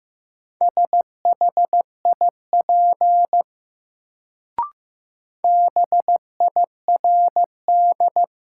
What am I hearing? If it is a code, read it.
Morse code, heard as a keyed tone, about 15 wpm, 700 Hz. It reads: SHIP BIRD
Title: Sentences from Top 400 Words - New Word Drill - 15wpm